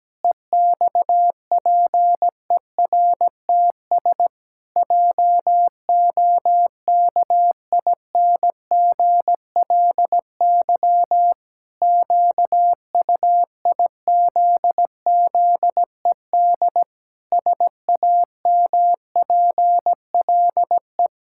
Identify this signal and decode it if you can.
Morse code, heard as a keyed tone, about 17 wpm, 700 Hz. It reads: EXPERTS JOKINGLY QUIZZED SAMPLE
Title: Pangram Challenge v2 - 17wpm